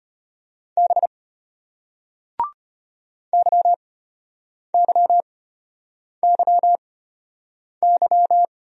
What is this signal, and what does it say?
Morse code, heard as a keyed tone, about 33 wpm, spaced as for 8 wpm, 700 Hz. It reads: B YYYY